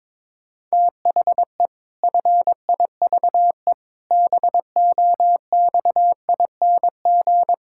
Morse code, heard as a keyed tone, about 22 wpm, 700 Hz. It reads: THE FIVE BOXING
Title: Pangram Challenge v2 - 22wpm